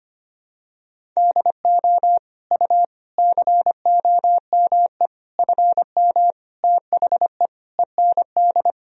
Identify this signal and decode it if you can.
Morse code, heard as a keyed tone, about 25 wpm, 700 Hz. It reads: DO U COME FM THE END
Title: Sentences from Top 200 Words and Abbreviations 25wpm